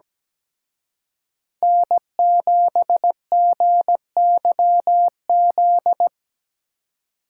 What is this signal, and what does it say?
Morse code, heard as a keyed tone, about 20 wpm, 700 Hz. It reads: N7GYZ
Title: Speed Racing - USA 1x3 Call Signs 17wpm